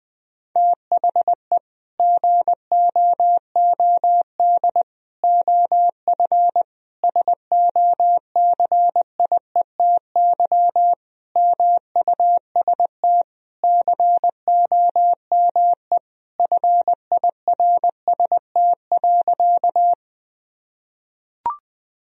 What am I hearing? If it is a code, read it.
Morse code, heard as a keyed tone, about 21 wpm, 700 Hz. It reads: THE GOOD OF SOCIETY MUST COME FIRST.
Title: Sentences from Top 1000 Words - Review Part 4 - 20wpm